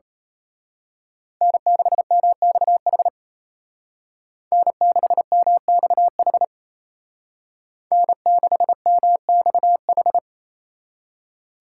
Morse code, heard as a keyed tone, about 32 wpm, 700 Hz. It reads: N6MXH N6MXH N6MXH
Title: Speed Racing - USA 1x3 Call Signs 28wpm